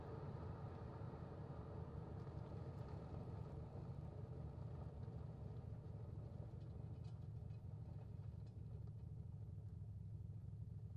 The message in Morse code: none